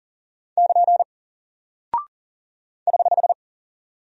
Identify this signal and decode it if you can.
Morse code, heard as a keyed tone, about 40 wpm, 700 Hz. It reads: <KN> <HH>
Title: Single Character (Letter:Number:Punctuation:Prosign) Alt 40wpm